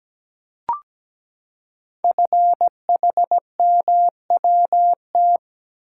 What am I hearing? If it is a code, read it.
Morse code, heard as a keyed tone, about 17 wpm, 700 Hz. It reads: FHMWT
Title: Code Groups - 5 Letters - 17wpm